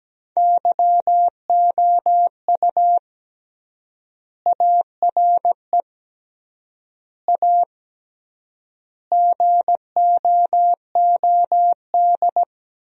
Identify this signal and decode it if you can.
Morse code, heard as a keyed tone, about 17 wpm, 700 Hz. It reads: YOU ARE A GOOD
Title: 3x Word-Spacing - Sentences from Top 200 Words - 17wpm